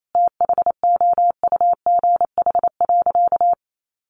Morse code, heard as a keyed tone, about 28 wpm, 700 Hz. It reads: THOUGH.
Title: Sentences from Top 1500 Words 28wpm